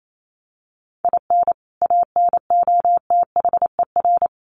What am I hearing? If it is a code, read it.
Morse code, heard as a keyed tone, about 28 wpm, 700 Hz. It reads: IN ANOTHER
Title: Sentences from Top 1100 Words 28wpm